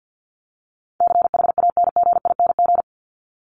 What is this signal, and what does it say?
Morse code, heard as a keyed tone, about 50 wpm, 700 Hz. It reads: CHANGING
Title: ING Suffix - Top 100 Words with Base Words 50wpm